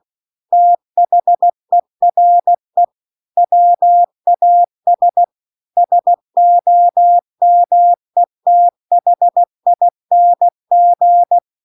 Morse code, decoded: THERE WAS SOMETHING

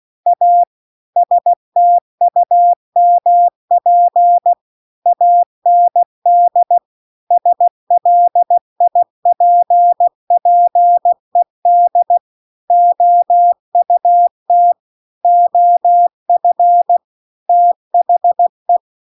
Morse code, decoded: A STUMP AND SLIPPED OUT OF THE

